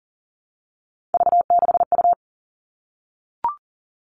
Morse code, heard as a keyed tone, about 40 wpm, 700 Hz. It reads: V6U
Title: Three Challenging Letter-Number Combinations 40wpm